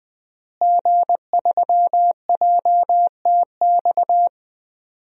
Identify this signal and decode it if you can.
Morse code, heard as a keyed tone, about 20 wpm, 700 Hz. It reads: G3JTX
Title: DXCC-100 Call Signs - 5 Characters - Repeated 3x - 20wpm